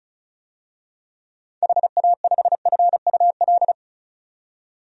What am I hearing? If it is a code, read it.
Morse code, heard as a keyed tone, about 34 wpm, 700 Hz. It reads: HA5FUL
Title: DXCC-100 Call Signs - 6 Characters - 35wpm